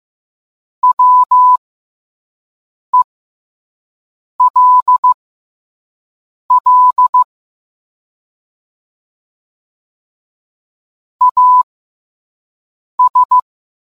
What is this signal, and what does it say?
Morse code, heard as a keyed tone, about 15 wpm, spaced as for 5 wpm, 1000 Hz. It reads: WELL AS